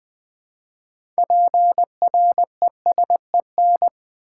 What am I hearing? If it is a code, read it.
Morse code, heard as a keyed tone, about 20 wpm, 690 Hz. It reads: PRESEN
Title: Build A Word - 3 Letters At A Time - 9-Letter Words - 20wpm